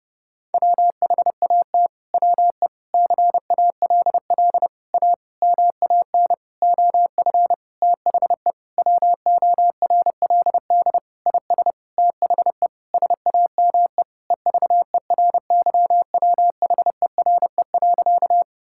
Morse code, heard as a keyed tone, about 32 wpm, 700 Hz. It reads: WHAT WE CALL A MAN OF THE WORLD IS THE SAME EVERYWHERE.
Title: Sentences from Top 1600 Words 30wpm